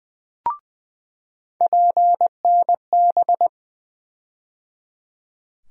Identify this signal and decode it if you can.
Morse code, heard as a keyed tone, about 20 wpm, 700 Hz. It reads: PNB
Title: USA Call Sign Postfixes - 3 Letters 20wpm